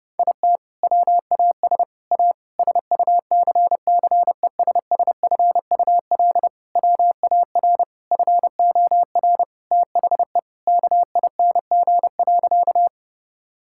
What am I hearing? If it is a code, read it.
Morse code, heard as a keyed tone, about 30 wpm, 700 Hz. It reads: IT WAS A SUCCESSFUL WAR FOR THE KING.
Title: Sentences from Top 1000 Words - Review Part 6 - 30wpm